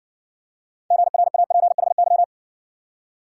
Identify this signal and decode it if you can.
Morse code, heard as a keyed tone, about 60 wpm, 700 Hz. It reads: GRAPHY